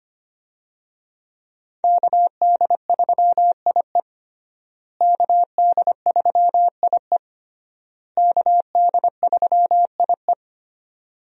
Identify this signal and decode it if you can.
Morse code, heard as a keyed tone, about 25 wpm, 700 Hz. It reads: KD3IE KD3IE KD3IE
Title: USA Call Signs - 5 Characters - Repeated 3x - 25wpm